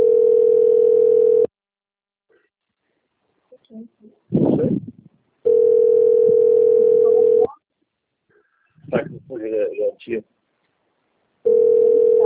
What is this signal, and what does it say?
Ringback tone (North America), 440 + 480 Hz, with speech.
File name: out- - - - - .22848